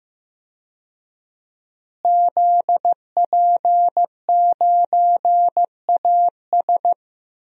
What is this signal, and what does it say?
Morse code, heard as a keyed tone, about 15 wpm, 700 Hz. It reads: ZP9AS
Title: DXCC-100 Call Signs - 5 Characters - Repeated 3x - 15wpm